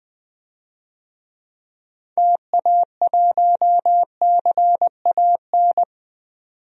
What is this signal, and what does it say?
Morse code, heard as a keyed tone, about 20 wpm, 700 Hz. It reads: TA1CAN
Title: DXCC-100 Call Signs - 6 Characters - 20wpm